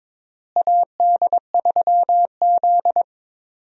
Morse code, heard as a keyed tone, about 22 wpm, 700 Hz. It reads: AD3Z
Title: USA Call Signs - 4 Characters - Repeated 2x - 22wpm